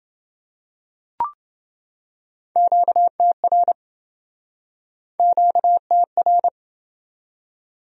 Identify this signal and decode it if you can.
Morse code, heard as a keyed tone, about 28 wpm, 700 Hz. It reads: QTR QTR